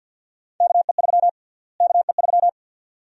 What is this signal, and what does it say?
Morse code, heard as a keyed tone, about 51 wpm, 700 Hz. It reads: KE3 KE3
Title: USA Call Sign Prefixes - Repeated 2x - 50wpm